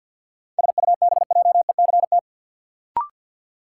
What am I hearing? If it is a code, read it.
Morse code, heard as a keyed tone, about 52 wpm, 700 Hz. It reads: SUBJECT
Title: Sentences from Top 800 Words - New Word Drill - 50wpm